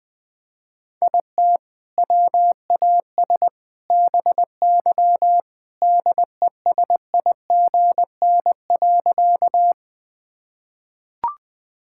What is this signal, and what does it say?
Morse code, heard as a keyed tone, about 20 wpm, 700 Hz. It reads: IT WAS BY DESIGN.
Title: Sentences from Top 1000 Words - Review Part 5 - 20wpm